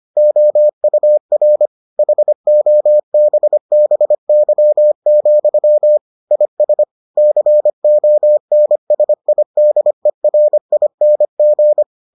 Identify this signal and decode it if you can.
Morse code, heard as a keyed tone, about 25 wpm, 600 Hz. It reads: OUR HOBBY, IS CONSIDERING